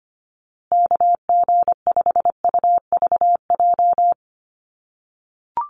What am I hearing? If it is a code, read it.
Morse code, heard as a keyed tone, about 25 wpm, 700 Hz. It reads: KG5UVJ
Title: USA Call Signs Repeated 3x 25wpm